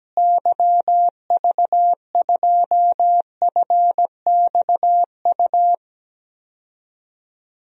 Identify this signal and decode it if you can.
Morse code, heard as a keyed tone, about 17 wpm, 700 Hz. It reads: YV2FXU
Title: DXCC-100 Call Signs - 6 Characters - Repeated 2x - 17wpm